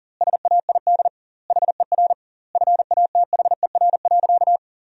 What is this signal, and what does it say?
Morse code, heard as a keyed tone, about 40 wpm, 700 Hz. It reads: SAID HER FATHER.
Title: 2x Word-Spacing - Sentences from Top 300 Words - 40wpm